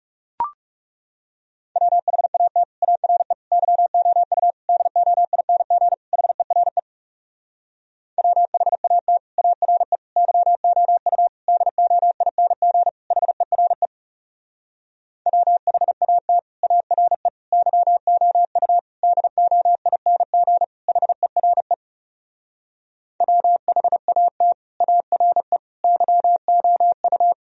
Morse code, heard as a keyed tone, about 40 wpm, 700 Hz. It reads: WHAT ARE YOU DOING HERE WHAT ARE YOU DOING HERE WHAT ARE YOU DOING HERE WHAT ARE YOU